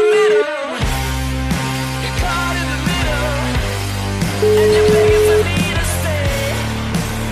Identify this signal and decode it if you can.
Ringback tone (Europe, 425 Hz), with noise.